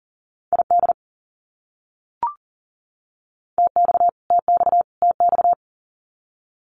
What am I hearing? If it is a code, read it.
Morse code, heard as a keyed tone, about 40 wpm, 700 Hz. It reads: ID TX TX TX
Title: US States Repeated 3x 40wpm